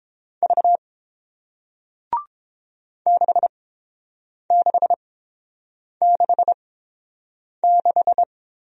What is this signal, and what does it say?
Morse code, heard as a keyed tone, about 30 wpm, spaced as for 7 wpm, 700 Hz. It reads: V 6666